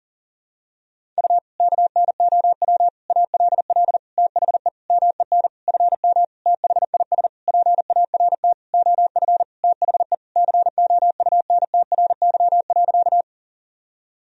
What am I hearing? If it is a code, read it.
Morse code, heard as a keyed tone, about 40 wpm, 700 Hz. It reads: U KNOW ALL THE MEN FM THIS PART OF THE COUNTRY.
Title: Sentences from Top 300 Words and Abbreviations 40wpm